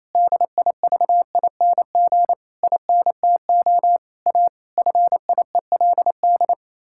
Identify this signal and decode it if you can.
Morse code, heard as a keyed tone, about 28 wpm, 700 Hz. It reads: DIVING INTO A FIELD
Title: Pangram Challenge v2 - 28wpm